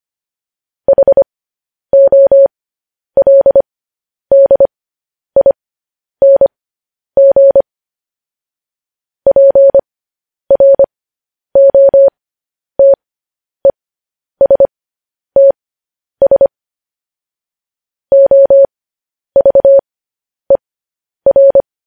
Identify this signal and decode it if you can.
Morse code, heard as a keyed tone, about 25 wpm, spaced as for 10 wpm, 550 Hz. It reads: HOLDING PROTESTS OVER
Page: news headlines in morse code at 10 wpm